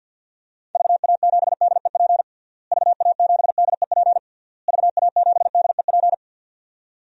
Morse code, heard as a keyed tone, about 53 wpm, 700 Hz. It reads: VA7DEP VA7DEP VA7DEP
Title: DXCC-100 Call Signs - 6 Characters - Repeated 3x - 50wpm